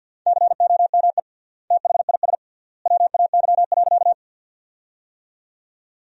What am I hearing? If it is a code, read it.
Morse code, heard as a keyed tone, about 53 wpm, 700 Hz. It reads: COME THIS WAY.
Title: 3x Word-Spacing - Sentences from Top 100 Words - 50wpm